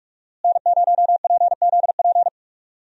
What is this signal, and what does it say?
Morse code, heard as a keyed tone, about 43 wpm, 700 Hz. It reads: N0PZP
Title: USA Call Signs - 5 Characters - Repeated 3x - 45wpm